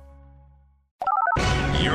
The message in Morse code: none